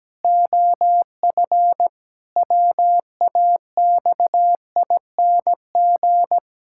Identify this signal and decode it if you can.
Morse code, heard as a keyed tone, about 17 wpm, 700 Hz. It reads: OF WAXING
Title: Pangram Challenge v2 - 17wpm